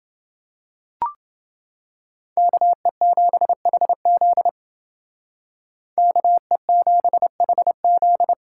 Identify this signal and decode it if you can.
Morse code, heard as a keyed tone, about 30 wpm, 700 Hz. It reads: KE7HZ KE7HZ